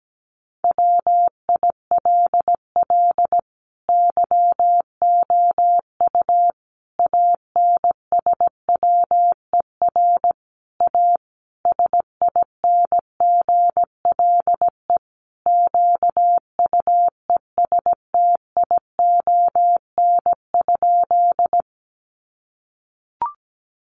Morse code, heard as a keyed tone, about 17 wpm, 700 Hz. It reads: WILL YOU ANSWER A SINGLE QUESTION?
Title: Sentences from Top 700 Words 17wpm